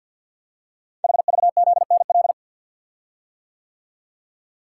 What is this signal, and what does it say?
Morse code, heard as a keyed tone, about 47 wpm, 700 Hz. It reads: HVZNL
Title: Code Groups - 5 Letters - 50wpm